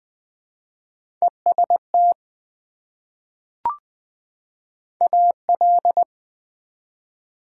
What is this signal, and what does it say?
Morse code, heard as a keyed tone, about 20 wpm, 700 Hz. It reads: EST AL
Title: Prefixes and Suffixes 20wpm